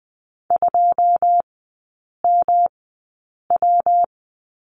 Morse code, heard as a keyed tone, about 20 wpm, spaced as for 8 wpm, 700 Hz. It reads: 2MW
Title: 159 Lesson 26e - Letter g - Three Characters 20wpm